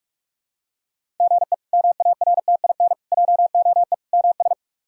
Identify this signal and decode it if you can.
Morse code, heard as a keyed tone, about 42 wpm, 700 Hz. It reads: GE MARTIN JOE MS
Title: K1USN SST QSO Practice - 45wpm